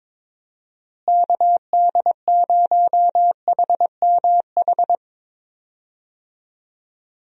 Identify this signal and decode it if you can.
Morse code, heard as a keyed tone, about 22 wpm, 700 Hz. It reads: KD0HMH
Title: USA Call Signs - 6 Characters - Repeated 2x - 22wpm